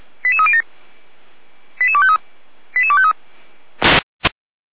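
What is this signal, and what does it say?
Five-tone selective call (ZVEI), heard as quick runs of five tones, tones of about 70 ms. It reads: 88307 88253 88253
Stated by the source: ZVEI five-tone call received on SDR by the Orlamünde fire brigade (feuerwehr-orlamuende).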